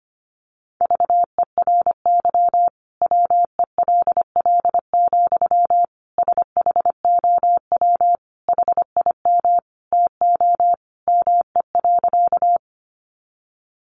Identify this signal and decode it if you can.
Morse code, heard as a keyed tone, about 25 wpm, 700 Hz. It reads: VERY WELL, SHOW HIM TO ME.
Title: Sentences from Top 200 Words 25wpm